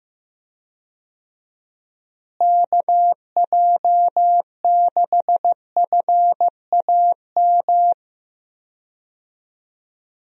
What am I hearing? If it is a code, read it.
Morse code, heard as a keyed tone, about 15 wpm, 700 Hz. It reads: KJ6FAM
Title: USA Call Signs - 6 Characters - Repeated 2x - 15wpm